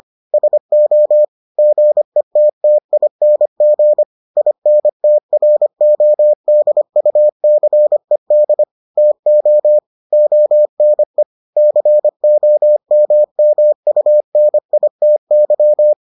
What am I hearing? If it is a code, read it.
Morse code, heard as a keyed tone, about 25 wpm, 600 Hz. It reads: SO GETTING INTRODUCED TO ONE COMMUNITY